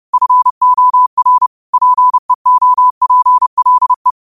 Morse code, heard as a keyed tone, about 30 wpm, 1000 Hz. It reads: FOR PEOPLE